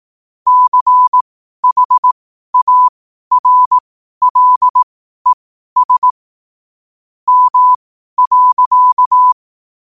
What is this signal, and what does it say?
Morse code, heard as a keyed tone, about 18 wpm, spaced as for 12 wpm, 1000 Hz. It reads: CHARLES M.
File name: qotdfiles / 12wpm